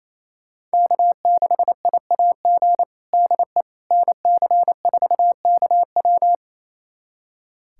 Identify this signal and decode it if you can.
Morse code, heard as a keyed tone, about 28 wpm, 700 Hz. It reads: K6IAG DE NC4KW
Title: USA Callsign DE Callsign - 5 Characters - 28wpm